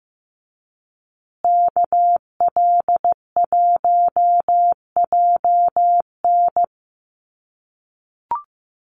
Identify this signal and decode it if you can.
Morse code, heard as a keyed tone, about 15 wpm, 700 Hz. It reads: KL1JN
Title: USA Call Signs Repeated 3x 15wpm